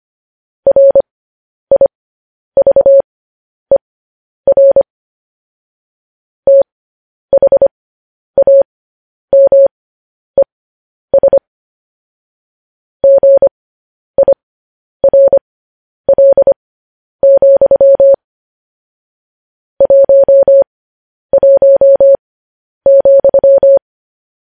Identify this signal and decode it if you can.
Morse code, heard as a keyed tone, about 25 wpm, spaced as for 10 wpm, 550 Hz. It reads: RIVER THAMES GIRL, 11,
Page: news headlines in morse code at 10 wpm